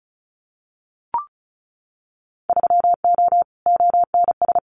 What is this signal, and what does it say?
Morse code, heard as a keyed tone, about 36 wpm, 700 Hz. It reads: 3O ONS
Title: Winter Field Day Exchanges 35wpm